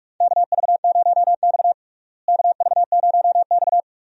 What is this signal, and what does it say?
Morse code, heard as a keyed tone, about 42 wpm, 700 Hz. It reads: KV0X KV0X
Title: USA Call Signs - 4 Characters - Repeated 2x - 45wpm